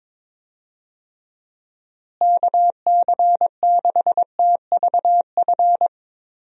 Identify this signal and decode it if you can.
Morse code, heard as a keyed tone, about 22 wpm, 700 Hz. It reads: KC6TVF
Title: USA Call Signs - 6 Characters - Repeated 2x - 22wpm